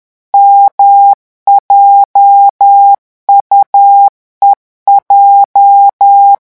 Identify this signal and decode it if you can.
Morse code, heard as a keyed tone, about 10 wpm, 800 Hz. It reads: MJUEJ
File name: morse mp3 / revision8-vitesse 10 mots